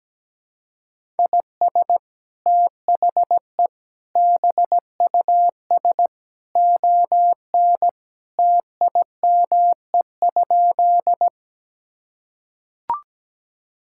Morse code, heard as a keyed tone, about 17 wpm, 700 Hz. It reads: IS THE BUS ON TIME?
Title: Sentences from Top 1000 Words - Review Part 6 - 17wpm